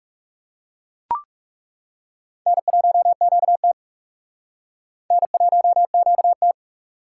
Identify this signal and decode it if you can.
Morse code, heard as a keyed tone, about 40 wpm, 700 Hz. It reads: N1QT N1QT